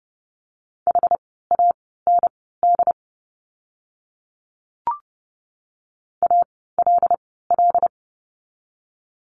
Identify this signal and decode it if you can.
Morse code, heard as a keyed tone, about 30 wpm, spaced as for 17 wpm, 700 Hz. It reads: HAND ALL
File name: course / 30:10wpm / 115 Lesson 19f - Letter w - Words 30wpm